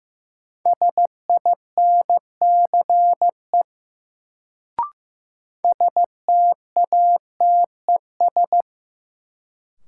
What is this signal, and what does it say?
Morse code, heard as a keyed tone, about 15 wpm, 700 Hz. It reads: SINCE STATES
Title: Sets of 2 Words - New Word Drill - 15wpm